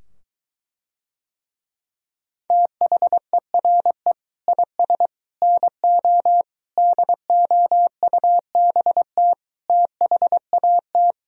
Morse code, heard as a keyed tone, about 23 wpm, 700 Hz. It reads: THERE IS NO DOUBT THAT